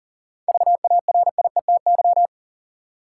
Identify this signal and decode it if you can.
Morse code, heard as a keyed tone, about 40 wpm, 700 Hz. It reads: VARIETY
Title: Seven Letter Words - 40wpm